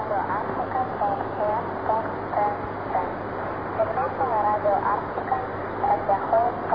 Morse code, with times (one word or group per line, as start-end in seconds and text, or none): none